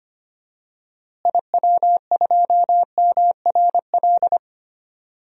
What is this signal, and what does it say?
Morse code, heard as a keyed tone, about 25 wpm, 700 Hz. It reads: IW2MRL
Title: DXCC-100 Call Signs - 6 Characters - Repeated 3x - 25wpm